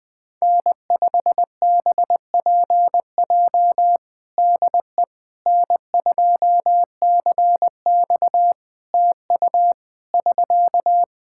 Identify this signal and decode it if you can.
Morse code, heard as a keyed tone, about 20 wpm, 700 Hz. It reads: N5BPJ DE N2CX TU <SK>
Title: K1USN SST QSO Practice 20wpm